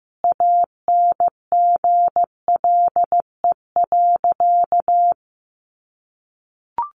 Morse code, heard as a keyed tone, about 14 wpm, 700 Hz. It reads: ANGLE.
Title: Sentences from Top 1500 Words 15wpm